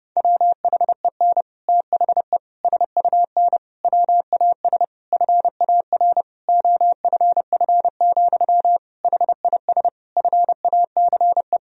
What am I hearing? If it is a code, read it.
Morse code, heard as a keyed tone, about 32 wpm, 700 Hz. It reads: WHEN THE SUN WAS FAR OFF, HIS FACE
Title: Sentences from Top 1000 Words - Review Part 3 - 30wpm